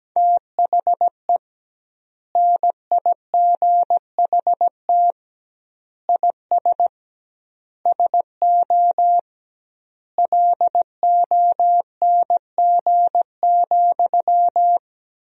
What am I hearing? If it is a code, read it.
Morse code, heard as a keyed tone, about 17 wpm, 700 Hz. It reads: THE NIGHT IS SO LONG,
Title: 2x Word-Spacing - Sentences from Top 500 Words - 17wpm